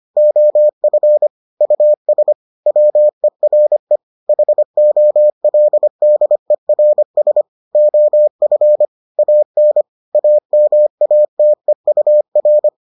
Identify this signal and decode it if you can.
Morse code, heard as a keyed tone, about 25 wpm, 600 Hz. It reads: OF US WERE HOLDERS OF AN AMATEUR